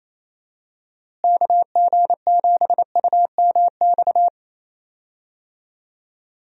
Morse code, heard as a keyed tone, about 28 wpm, 700 Hz. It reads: KG7UMX
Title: USA Call Signs - 6 Characters - Repeated 3x - 28wpm